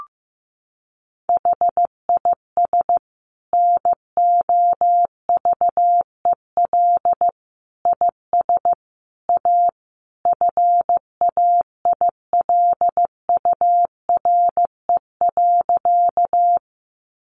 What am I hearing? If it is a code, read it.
Morse code, heard as a keyed tone, about 14 wpm, 700 Hz. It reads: HIS NOVEL IS A FAILURE.
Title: Sentences from Top 1300 Words 15wpm